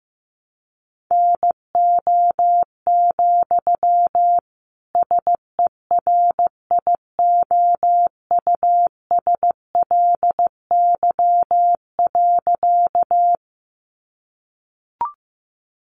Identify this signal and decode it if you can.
Morse code, heard as a keyed tone, about 15 wpm, 700 Hz. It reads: NO, SERIOUSLY.
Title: Sentences from Top 1200 Words 15wpm